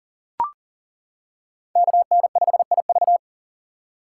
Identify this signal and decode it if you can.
Morse code, heard as a keyed tone, about 38 wpm, 690 Hz. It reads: KN5IV
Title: Code Groups - 5 Letter:Numbers - 40wpm